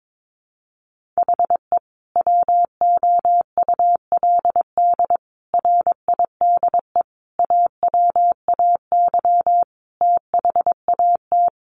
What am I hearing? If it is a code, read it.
Morse code, heard as a keyed tone, about 22 wpm, 700 Hz. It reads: HE WOULD RIDE AWAY THAT
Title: Sentences from Top 1200 Words 22wpm